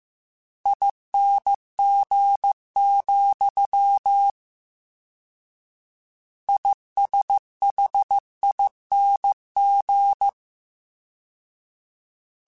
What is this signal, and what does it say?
Morse code, heard as a keyed tone, about 15 wpm, 800 Hz. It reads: ING, ISHING
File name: Combinations / Combinations-mp3s / Combinations@13